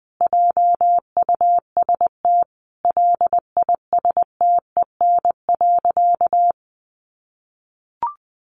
Morse code, heard as a keyed tone, about 20 wpm, 700 Hz. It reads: JUST LISTEN.
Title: Sentences from Top 500 Words 20wpm